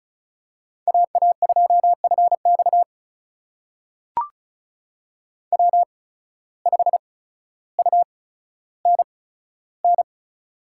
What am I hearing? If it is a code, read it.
Morse code, heard as a keyed tone, about 34 wpm, spaced as for 15 wpm, 700 Hz. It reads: AA2FX W5UNN
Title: USA Call Signs - 5 Characters - 8x Character Spacing - 35wpm